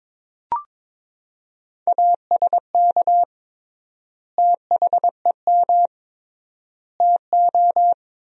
Morse code, heard as a keyed tone, about 22 wpm, 700 Hz. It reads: ASK THEM TO
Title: 3x Word-Spacing - Sentences from Top 400 Words - 22wpm